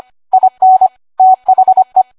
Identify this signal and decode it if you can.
Morse code, heard as a keyed tone, about 25 wpm, 750 Hz. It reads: IN THE